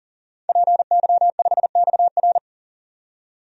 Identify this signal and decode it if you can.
Morse code, heard as a keyed tone, about 37 wpm, 700 Hz. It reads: PY5XR
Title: DXCC-100 Call Signs - 5 Characters - 40wpm